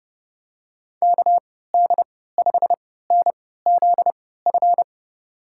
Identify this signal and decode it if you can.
Morse code, heard as a keyed tone, about 31 wpm, spaced as for 17 wpm, 700 Hz. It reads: KD5NZF
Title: USA Call Signs - 6 Characters - 3x Character Spacing - 30wpm